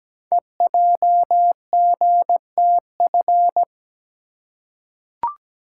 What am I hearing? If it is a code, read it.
Morse code, heard as a keyed tone, about 17 wpm, 700 Hz. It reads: EJGTF
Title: Code Groups - 5 Letters - 17wpm